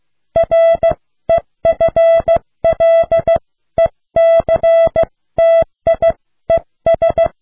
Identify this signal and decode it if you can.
Morse code, heard as a keyed tone, about 15 wpm, spaced as for 13 wpm, 660 Hz. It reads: REFLECTIES